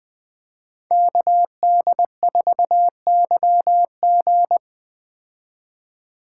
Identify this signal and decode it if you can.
Morse code, heard as a keyed tone, about 20 wpm, 700 Hz. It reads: KD4YG
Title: Speed Racing - USA 2x2 Call Signs 17wpm